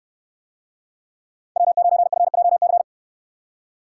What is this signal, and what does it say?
Morse code, heard as a keyed tone, about 69 wpm, 700 Hz. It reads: W9FJC